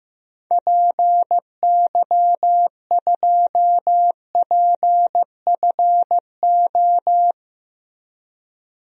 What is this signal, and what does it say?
Morse code, heard as a keyed tone, about 15 wpm, 700 Hz. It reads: PY2PFO
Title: DXCC-100 Call Signs - 6 Characters - Repeated 2x - 15wpm